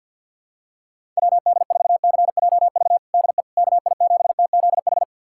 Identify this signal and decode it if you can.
Morse code, heard as a keyed tone, about 51 wpm, 700 Hz. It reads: WD4CJV DE KI7TZH
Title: USA Callsign DE Callsign - 6 Characters - 50wpm